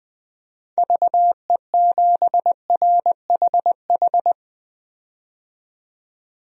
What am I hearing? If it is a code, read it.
Morse code, heard as a keyed tone, about 20 wpm, 700 Hz. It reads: VE7RHH
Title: DXCC-100 Call Signs - 6 Characters - 20wpm